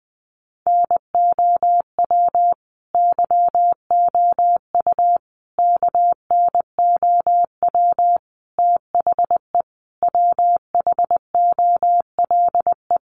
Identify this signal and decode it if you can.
Morse code, heard as a keyed tone, about 20 wpm, 700 Hz. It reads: NOW YOU KNOW THE WHOLE
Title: Sentences from Top 1200 Words 20wpm